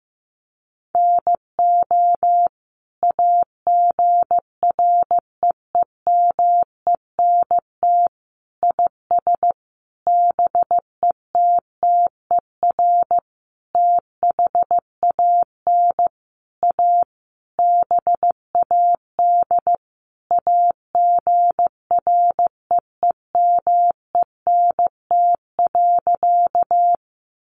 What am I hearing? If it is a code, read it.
Morse code, heard as a keyed tone, about 15 wpm, 700 Hz. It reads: NO AGREEMENT IS BETTER THAN A BAD AGREEMENT.
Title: Sentences from Top 1500 Words 15wpm